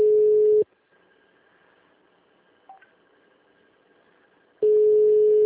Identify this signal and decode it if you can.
Ringback tone (Europe, 425 Hz).
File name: out- -2013- - - .45541